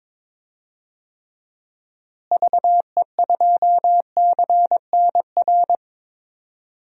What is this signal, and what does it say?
Morse code, heard as a keyed tone, about 22 wpm, 700 Hz. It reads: VE2CNR
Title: DXCC-100 Call Signs - 6 Characters - Repeated 3x - 22wpm